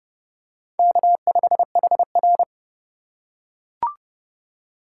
Morse code, heard as a keyed tone, about 27 wpm, 700 Hz. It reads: K5HR